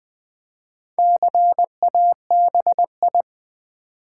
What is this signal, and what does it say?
Morse code, heard as a keyed tone, about 20 wpm, 700 Hz. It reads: CABI
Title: Build A Word - 1 Letter At A Time - 5-Letter Words - 20wpm